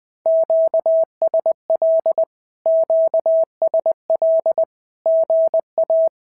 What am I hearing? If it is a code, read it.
Morse code, heard as a keyed tone, about 20 wpm, 650 Hz. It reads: QSL QSL GA
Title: SOTA QSO Practice 20wpm